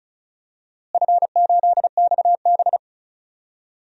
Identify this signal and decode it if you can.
Morse code, heard as a keyed tone, about 34 wpm, 700 Hz. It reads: F8XB
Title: DXCC-100 Call Signs - 4 Characters - Repeated 3x - 35wpm